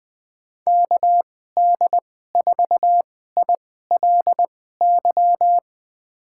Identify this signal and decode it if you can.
Morse code, heard as a keyed tone, about 20 wpm, spaced as for 14 wpm, 700 Hz. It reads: KD4ILY
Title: USA Call Signs - 6 Characters - 2x Character Spacing - 20wpm